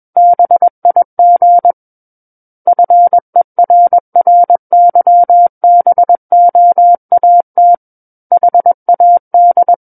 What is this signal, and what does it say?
Morse code, heard as a keyed tone, about 21 wpm, 700 Hz. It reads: BIG FERRYBOAT HAD
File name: Book One / 12-Chapter 12 - 21 WPM